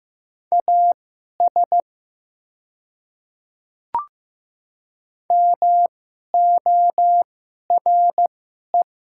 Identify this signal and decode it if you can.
Morse code, heard as a keyed tone, about 15 wpm, spaced as for 11 wpm, 700 Hz. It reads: AS MORE
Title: 2x Character Spacing - Top 100 Words - 15wpm